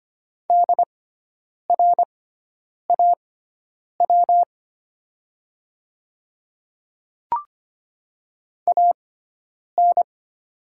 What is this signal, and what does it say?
Morse code, heard as a keyed tone, about 25 wpm, spaced as for 9 wpm, 700 Hz. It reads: DRAW AN
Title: 6x Character Spacing - Top 300 Words - 25wpm